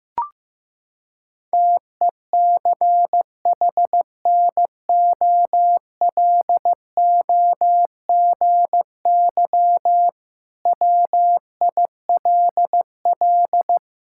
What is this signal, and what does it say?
Morse code, heard as a keyed tone, about 15 wpm, 690 Hz. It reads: TECHNOLOGY WILL
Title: Sentences from Top 1000 Words - Review Part 5 - 15wpm